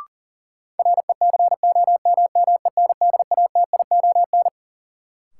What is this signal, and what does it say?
Morse code, heard as a keyed tone, about 40 wpm, 700 Hz. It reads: RECOMMENDATION
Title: Sentences from Top 1400 Words - New Word Drill Alt - 40wpm